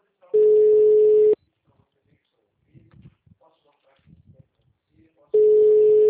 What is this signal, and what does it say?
Ringback tone (Europe, 425 Hz), with speech.
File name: out- -2002- - - .1416